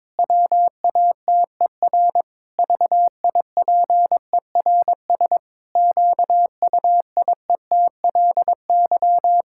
Morse code, heard as a keyed tone, about 22 wpm, 700 Hz. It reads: WATER VIPERS QUIETLY
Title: Pangram Challenge v2 - 22wpm